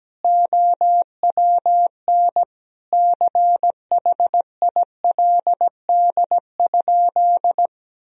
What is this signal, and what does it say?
Morse code, heard as a keyed tone, about 17 wpm, 700 Hz. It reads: OWN CHILD?
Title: Sentences from Top 1000 Words - Review Part 2 - 17wpm